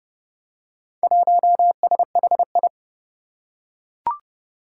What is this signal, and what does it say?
Morse code, heard as a keyed tone, about 30 wpm, 700 Hz. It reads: 1SHI
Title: Four Challenging Letter-Number Combinations 30wpm